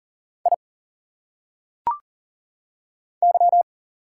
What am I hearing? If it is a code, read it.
Morse code, heard as a keyed tone, about 38 wpm, 700 Hz. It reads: I Y
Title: Single Letters - Phonetic Pronunciation 40wpm